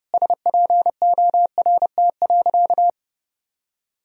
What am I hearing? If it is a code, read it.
Morse code, heard as a keyed tone, about 32 wpm, 700 Hz. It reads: SPORT.
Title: Sentences from Top 1000 Words - Review Part 4 - 30wpm